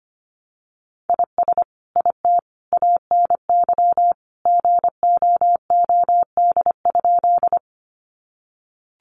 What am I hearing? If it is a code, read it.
Morse code, heard as a keyed tone, about 25 wpm, 700 Hz. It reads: IS IT ANY GOOD?
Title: Sentences from Top 200 Words 25wpm